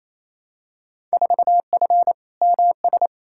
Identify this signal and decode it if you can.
Morse code, heard as a keyed tone, about 28 wpm, 700 Hz. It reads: 4F MS
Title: Field Day Exchanges Encore - 28wpm